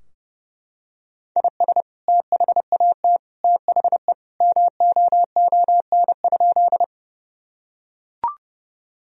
Morse code, heard as a keyed tone, about 32 wpm, 700 Hz. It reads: IS THAT THE MOON?
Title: Sentences from Top 1000 Words - Review Part 3 - 30wpm